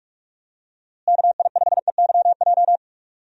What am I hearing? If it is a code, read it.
Morse code, heard as a keyed tone, about 46 wpm, 700 Hz. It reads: KI5EYJ
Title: USA Call Signs - 6 Characters - Repeated 2x - 45wpm